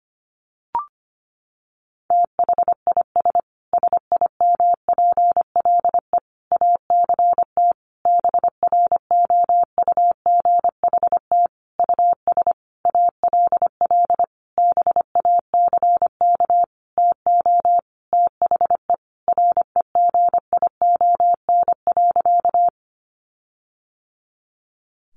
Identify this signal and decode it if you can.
Morse code, heard as a keyed tone, about 25 wpm, 700 Hz. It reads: THIS SIMPLE ACT BROUGHT US ALL BACK TO THE REGION.
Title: Sentences from Top 1500 Words 25wpm